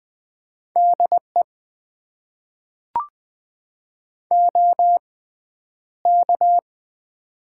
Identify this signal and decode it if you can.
Morse code, heard as a keyed tone, about 20 wpm, spaced as for 10 wpm, 700 Hz. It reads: DE OK
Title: Top 90 QSO Elements - 6x Character Spacing - 20wpm